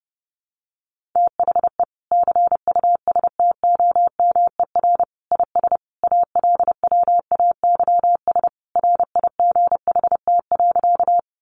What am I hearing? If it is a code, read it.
Morse code, heard as a keyed tone, about 30 wpm, 700 Hz. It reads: THE CUSTOMER IS ALWAYS RIGHT.
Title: Sentences from Top 1100 Words 30wpm